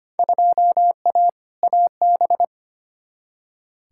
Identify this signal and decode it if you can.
Morse code, heard as a keyed tone, about 25 wpm, 700 Hz. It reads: 2A AB
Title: Field Day Exchanges - 25wpm